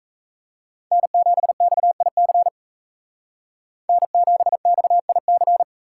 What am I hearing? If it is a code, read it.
Morse code, heard as a keyed tone, about 37 wpm, 700 Hz. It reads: N7XIC N7XIC